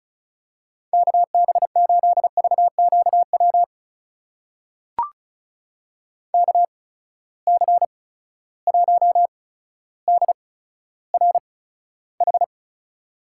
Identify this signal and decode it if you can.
Morse code, heard as a keyed tone, about 36 wpm, 700 Hz. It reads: K B 8 V Q W K C 1 D R H
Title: USA Call Signs - 8x Character Spacing - 35wpm